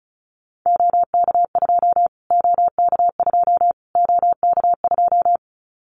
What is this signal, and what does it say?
Morse code, heard as a keyed tone, about 34 wpm, 700 Hz. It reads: OK2 OK2 OK2
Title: Callsign Prefixes for Top 25 DXCC - Repeated 3x - 35wpm